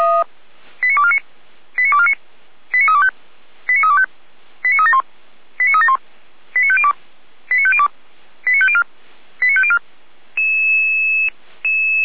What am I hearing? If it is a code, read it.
Five-tone selective call (ZVEI), heard as quick runs of five tones, tones of about 70 ms. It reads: 89249 89249 89325 89325 89371 89371 89502 89502 89504 89504